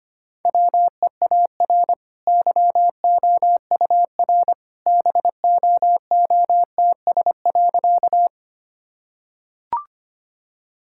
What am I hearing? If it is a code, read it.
Morse code, heard as a keyed tone, about 25 wpm, 700 Hz. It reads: WEAR YOUR BOOTS.
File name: Easy Sentences with Hints - 25wpm